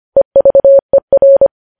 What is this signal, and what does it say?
Morse code, heard as a keyed tone, about 25 wpm, 550 Hz. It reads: EVER
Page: news headlines in morse code at 25 wpm